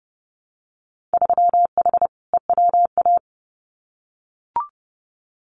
Morse code, heard as a keyed tone, about 32 wpm, 700 Hz. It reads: 3H EWA
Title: Winter Field Day Exchanges Encore 30wpm